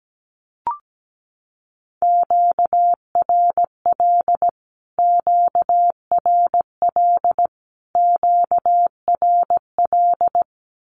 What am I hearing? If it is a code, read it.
Morse code, heard as a keyed tone, about 17 wpm, 700 Hz. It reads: QRL QRL QRL
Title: Q-Codes Repeated 3x 17wpm